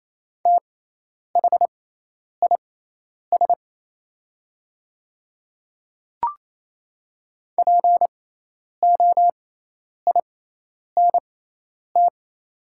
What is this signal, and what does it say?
Morse code, heard as a keyed tone, about 28 wpm, spaced as for 10 wpm, 700 Hz. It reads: THIS POINT
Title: 6x Character Spacing - Top 200 Words - 28wpm